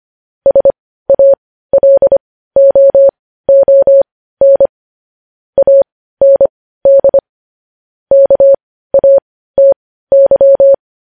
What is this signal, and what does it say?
Morse code, heard as a keyed tone, about 25 wpm, spaced as for 15 wpm, 550 Hz. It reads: SALOON AND KATY
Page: news-headlines-in-morse-code-at-15-wpm